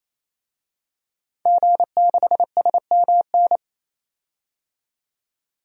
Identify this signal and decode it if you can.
Morse code, heard as a keyed tone, about 28 wpm, 700 Hz. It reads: G6SMN
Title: DXCC-100 Call Signs - 5 Characters - 28wpm